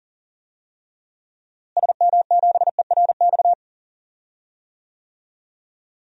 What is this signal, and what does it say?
Morse code, heard as a keyed tone, about 40 wpm, 700 Hz. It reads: SM7ERX
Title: DXCC-100 Call Signs - 6 Characters - Repeated 3x - 40wpm